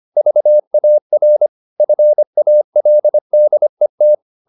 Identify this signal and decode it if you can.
Morse code, heard as a keyed tone, about 25 wpm, 600 Hz. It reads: VAR FALDET
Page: Ferie 25 wpm